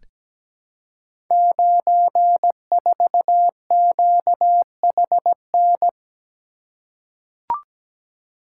Morse code, heard as a keyed tone, about 17 wpm, 700 Hz. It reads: 94QHN
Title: Code Groups - 5 Letter:Numbers - 17wpm